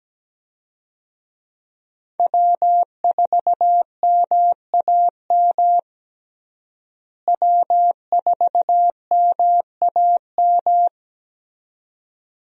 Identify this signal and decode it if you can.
Morse code, heard as a keyed tone, about 17 wpm, 700 Hz. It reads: W4MAM W4MAM
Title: USA Call Signs - 5 Characters - Repeated 2x - 17wpm